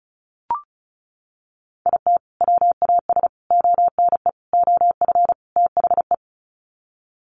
Morse code, heard as a keyed tone, about 34 wpm, 700 Hz. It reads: IT WAS ONE OF THE
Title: Sets of 5 Words 35wpm